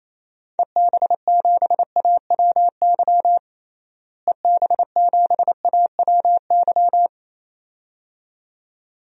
Morse code, heard as a keyed tone, about 27 wpm, 700 Hz. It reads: EB7AWY EB7AWY
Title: DXCC-100 Call Signs - 6 Characters - Repeated 2x - 28wpm